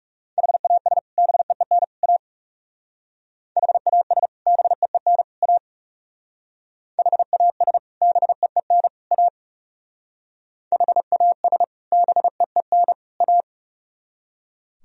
Morse code, heard as a keyed tone, about 37 wpm, 700 Hz. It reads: HAS BEEN A HAS BEEN A HAS BEEN A HAS BEEN A
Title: Speed-Racing - Sets of 3 Words 30wpm